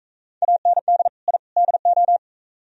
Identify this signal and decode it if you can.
Morse code, heard as a keyed tone, about 44 wpm, 700 Hz. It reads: AND I DO